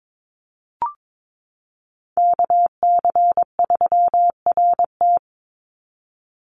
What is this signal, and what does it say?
Morse code, heard as a keyed tone, about 22 wpm, 700 Hz. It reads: KC3RT
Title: NA SOTA Chaser Callsigns 22wpm